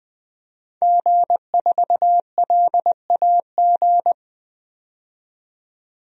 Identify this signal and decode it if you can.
Morse code, heard as a keyed tone, about 20 wpm, 700 Hz. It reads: G4LAG
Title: DXCC-100 Call Signs - 5 Characters - 20wpm